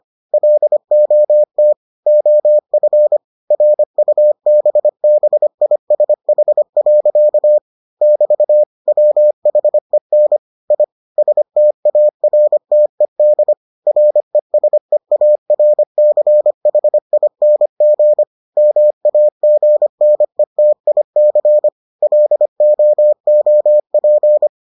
Morse code, heard as a keyed tone, about 25 wpm, 600 Hz. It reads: LOT OF RUBBISH. <BT> WHEN I STARTED RESEARCHING MAGNETIC LOOP